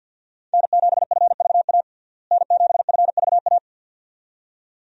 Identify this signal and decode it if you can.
Morse code, heard as a keyed tone, about 46 wpm, 700 Hz. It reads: N7FVA N7FVA
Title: USA Call Signs - 5 Characters - Repeated 2x - 50wpm